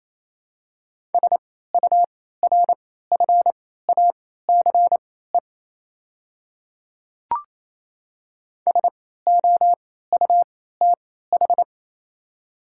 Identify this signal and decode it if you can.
Morse code, heard as a keyed tone, about 28 wpm, spaced as for 16 wpm, 700 Hz. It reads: SURFACE SOUTH
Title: 3x Character Spacing - Top 500 Words - 28wpm